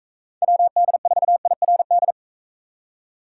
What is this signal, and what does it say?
Morse code, heard as a keyed tone, about 45 wpm, 700 Hz. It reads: WD4IRD